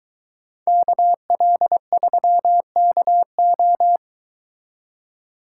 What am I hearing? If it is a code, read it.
Morse code, heard as a keyed tone, about 23 wpm, 700 Hz. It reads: KL3KO